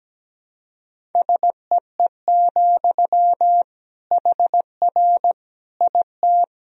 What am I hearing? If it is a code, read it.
Morse code, heard as a keyed tone, about 17 wpm, 700 Hz. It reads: SEE, HR IT
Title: Sentences from Top 200 Words and Abbreviations 17wpm